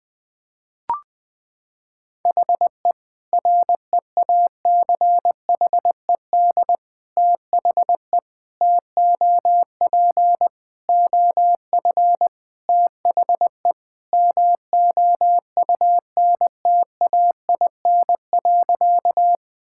Morse code, heard as a keyed tone, about 20 wpm, 700 Hz. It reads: HE REACHED THE TOP OF THE MOUNTAIN.
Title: Sentences with 3-Word Phrases - 20wpm